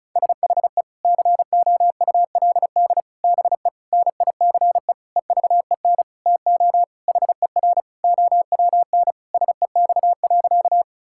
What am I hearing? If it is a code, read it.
Morse code, heard as a keyed tone, about 36 wpm, 700 Hz. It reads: SHE COULD BE NICE EVEN TO HER OWN SEX.
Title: Sentences from Top 1600 Words 35wpm